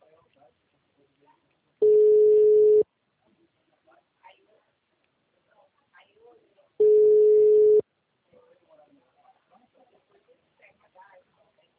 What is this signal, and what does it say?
Ringback tone (Europe, 425 Hz), with speech.